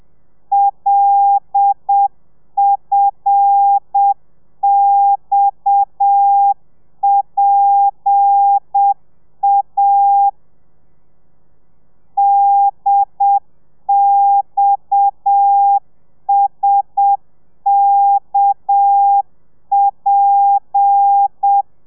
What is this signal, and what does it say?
Morse code, heard as a keyed tone, about 7 wpm, 800 Hz. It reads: LFXPA DXSKP
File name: telegrafia / i2tfj / Lez7 30